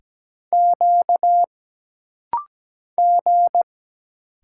Morse code, heard as a keyed tone, about 17 wpm, 700 Hz. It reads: Q G
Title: Single Letters - Rapid-Fire 17wpm